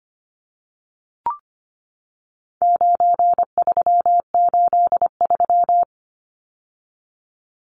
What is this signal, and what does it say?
Morse code, heard as a keyed tone, about 25 wpm, 700 Hz. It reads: 9383
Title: Numbers - 4-Digit 25wpm